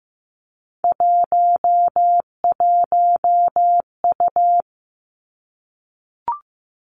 Morse code, heard as a keyed tone, about 15 wpm, 700 Hz. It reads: 11U
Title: Three Challenging Letter-Number Combinations 15wpm